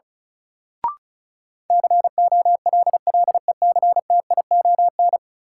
Morse code, heard as a keyed tone, about 36 wpm, 700 Hz. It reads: COLLECTION
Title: Sentences from Top 1200 Words - New Word Drill Alt - 35wpm